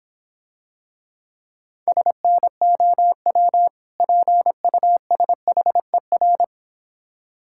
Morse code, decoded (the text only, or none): SNOW PUSHER